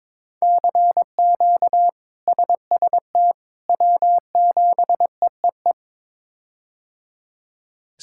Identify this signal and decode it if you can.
Morse code, heard as a keyed tone, about 21 wpm, 700 Hz. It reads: CQ SST W7EEE
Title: K1USN SST QSO Practice - 22wpm